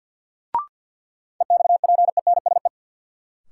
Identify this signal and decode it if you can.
Morse code, heard as a keyed tone, about 51 wpm, 700 Hz. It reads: EXPENSE
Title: Sets of 4 Words - New Word Drill - 50wpm